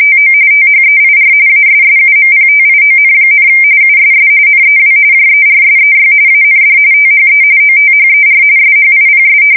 RTTY, about 45.45 baud. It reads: TO THEIR ASSUMPTION, USS NEW JERSEY AND THE OTHER SHIPS
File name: fsk170 / battleshipnj